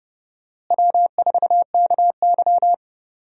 Morse code, heard as a keyed tone, about 30 wpm, 700 Hz. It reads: W4KY
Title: USA Call Signs - 4 Characters - Repeated 2x - 30wpm